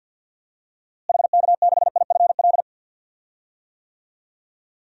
Morse code, heard as a keyed tone, about 52 wpm, 700 Hz. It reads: HK6IFL
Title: DXCC-100 Call Signs - 6 Characters - Repeated 2x - 50wpm